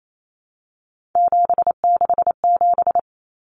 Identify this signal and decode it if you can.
Morse code, heard as a keyed tone, about 28 wpm, 700 Hz. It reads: 767
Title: Numbers - 3-Digit 28wpm